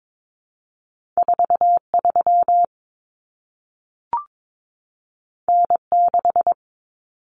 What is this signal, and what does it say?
Morse code, heard as a keyed tone, about 22 wpm, 700 Hz. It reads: 43 N6
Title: Common Cut Numbers - 2-Digit 22wpm